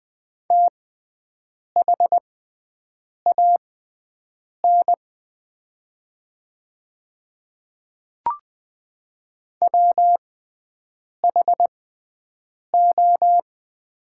Morse code, heard as a keyed tone, about 20 wpm, spaced as for 7 wpm, 700 Hz. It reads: THAN WHO
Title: 6x Character Spacing - Top 100 Words - 20wpm